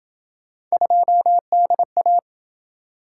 Morse code, heard as a keyed tone, about 27 wpm, 700 Hz. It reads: 2DA